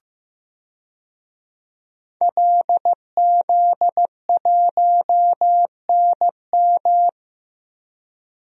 Morse code, heard as a keyed tone, about 15 wpm, 700 Hz. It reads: LZ1NM
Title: DXCC-100 Call Signs - 5 Characters - Repeated 3x - 15wpm